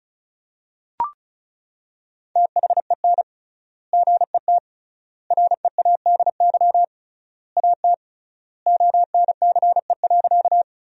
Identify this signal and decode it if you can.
Morse code, heard as a keyed tone, about 34 wpm, 700 Hz. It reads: THEN GET READY AT ONCE.
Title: 3x Word-Spacing - Sentences from Top 400 Words - 35wpm